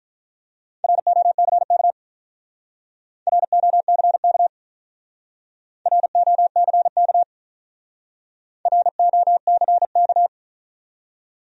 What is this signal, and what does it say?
Morse code, heard as a keyed tone, about 48 wpm, 700 Hz. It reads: ROCK ROCK ROCK ROCK